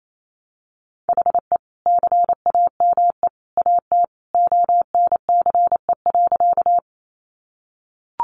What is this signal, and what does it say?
Morse code, heard as a keyed tone, about 28 wpm, 700 Hz. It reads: HE CAME AT ONCE.
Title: Sentences from Top 300 Words 28wpm